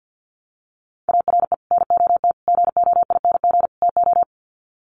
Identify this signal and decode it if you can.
Morse code, heard as a keyed tone, about 50 wpm, 700 Hz. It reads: ARE NOT GOING TO
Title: Sets of 4 Words 50wpm